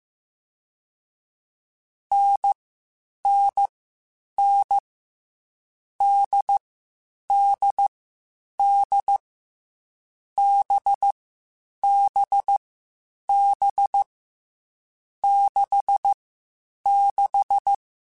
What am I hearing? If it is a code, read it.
Morse code, heard as a keyed tone, about 15 wpm, 800 Hz. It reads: N N N D D D B B B 6 6
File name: Short Runs / Short Runs-mp3s / NDB NDB6 @13